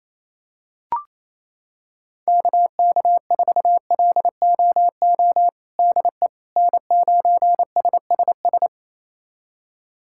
Morse code, heard as a keyed tone, about 28 wpm, 700 Hz. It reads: KK4LOO DE N9SSS
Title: USA Callsign DE Callsign 28wpm